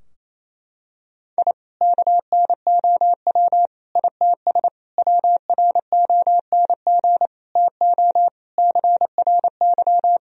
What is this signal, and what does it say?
Morse code, heard as a keyed tone, about 28 wpm, 700 Hz. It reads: I KNOW ITS WRONG TO CRY
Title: Sentences from Top 1000 Words - Review Part 2 - 28wpm